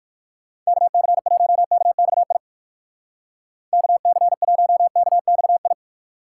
Morse code, heard as a keyed tone, about 58 wpm, 700 Hz. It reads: KC1KXI KC1KXI